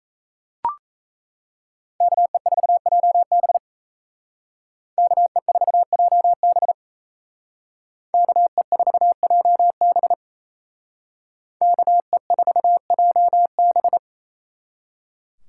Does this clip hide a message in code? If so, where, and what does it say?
Yes: Morse code from 2.0 to 14.0 s, heard as a keyed tone, about 36 wpm, 700 Hz. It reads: KE4JB KE4JB KE4JB KE4JB